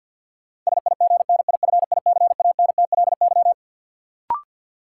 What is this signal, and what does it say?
Morse code, heard as a keyed tone, about 48 wpm, 700 Hz. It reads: SIGNIFICANTLY
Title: Sentences from Top 1500 Words - New Word Drill - 50wpm